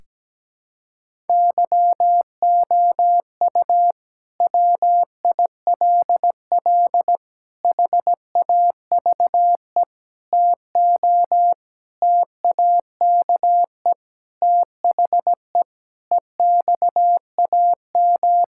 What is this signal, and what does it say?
Morse code, heard as a keyed tone, about 17 wpm, 700 Hz. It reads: YOU WILL HAVE TO TAKE THE EXAM
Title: Sentences from Top 1000 Words - Review Part 5 - 17wpm